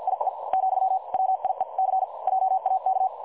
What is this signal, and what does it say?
Morse code, heard as a keyed tone, about 50 wpm, spaced as for 43 wpm, 760 Hz. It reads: IE , WIEC JAK
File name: dlazaawansowanych / nagraniazpasma / websdr recording 3565.0kHz